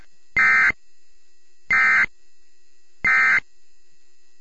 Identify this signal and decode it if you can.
Emergency Alert System (SAME) tones, heard as harsh buzzing bursts. It is heard from 0.3 to 3.4 s.